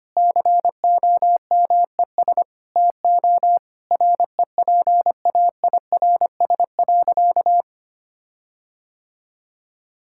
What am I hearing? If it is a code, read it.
Morse code, heard as a keyed tone, about 25 wpm, 700 Hz. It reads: COMES TO REPAIRS.
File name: practice / 25wpm / Difficult Sentences with Hints - 25wpm